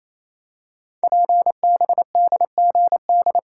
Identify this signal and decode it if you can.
Morse code, heard as a keyed tone, about 28 wpm, 700 Hz. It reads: PBDGD
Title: Code Groups - 5 Letters - 28wpm